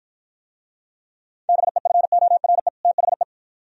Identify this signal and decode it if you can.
Morse code, heard as a keyed tone, about 53 wpm, 700 Hz. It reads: BEFORE THE